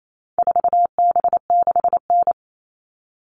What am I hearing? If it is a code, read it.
Morse code, heard as a keyed tone, about 28 wpm, 700 Hz. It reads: 4B6N
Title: All Cut Numbers - 4-Digit 28wpm